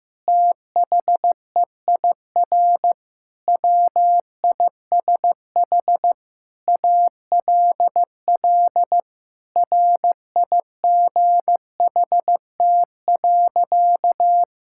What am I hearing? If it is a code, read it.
Morse code, heard as a keyed tone, about 15 wpm, 700 Hz. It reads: THEIR WISH ALL RIGHT.
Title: Sentences from Top 1600 Words 15wpm